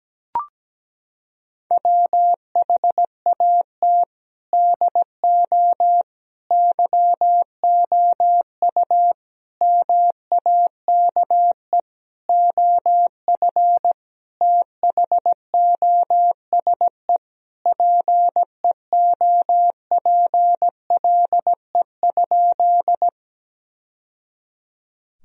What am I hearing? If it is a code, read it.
Morse code, heard as a keyed tone, about 17 wpm, 700 Hz. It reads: WHAT DO YOU MAKE OF THOSE PEOPLE?
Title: Sentences from Top 1000 Words - Review Part 1 - 17wpm